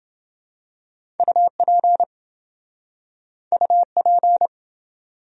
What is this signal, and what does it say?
Morse code, heard as a keyed tone, about 28 wpm, 700 Hz. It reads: UP UP